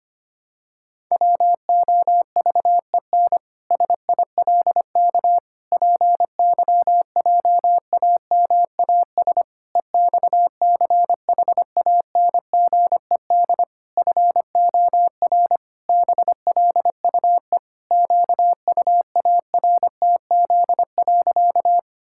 Morse code, heard as a keyed tone, about 25 wpm, 700 Hz. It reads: WOVEN SILK PYJAMAS EXCHANGED FOR BLUE QUARTZ.
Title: Pangram Challenge v2 - 25wpm